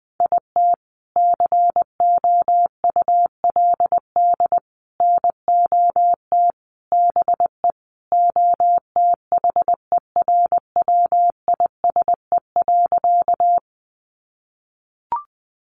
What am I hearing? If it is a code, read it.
Morse code, heard as a keyed tone, about 20 wpm, 700 Hz. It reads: IT COULD NOT BE OTHERWISE.
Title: Sentences from Top 1200 Words 20wpm